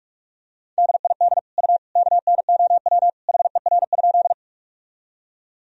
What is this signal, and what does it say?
Morse code, heard as a keyed tone, about 45 wpm, 700 Hz. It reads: DID U KNOW HER?
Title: Sentences from Top 100 Words and Abbreviations 45wpm